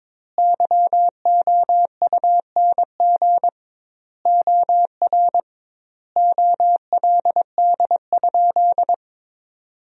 Morse code, heard as a keyed tone, about 22 wpm, 700 Hz. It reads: YOUNG OR OLD?
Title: 2x Word-Spacing - Sentences from Top 400 Words - 22wpm